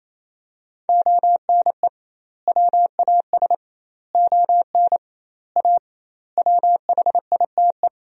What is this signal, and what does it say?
Morse code, heard as a keyed tone, about 28 wpm, 700 Hz. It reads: ONE WAS ON A WHITE
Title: 2x Word-Spacing - Sentences from Top 300 Words - 28wpm